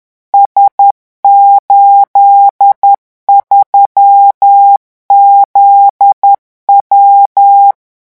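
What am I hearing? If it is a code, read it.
Morse code, heard as a keyed tone, about 10 wpm, 800 Hz. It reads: S83ZW
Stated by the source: French Morse code revision lesson 18 at 10 words per minute.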